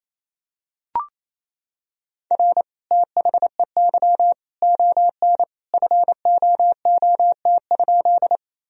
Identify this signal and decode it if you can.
Morse code, heard as a keyed tone, about 28 wpm, 700 Hz. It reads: R THEY ON FOOT?
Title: Sentences from Top 500 Words and Abbreviations 28wpm